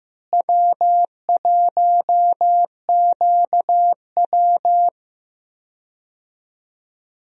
Morse code, heard as a keyed tone, about 15 wpm, 700 Hz. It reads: W1QW
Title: USA Call Signs - 4 Characters - Repeated 2x - 15wpm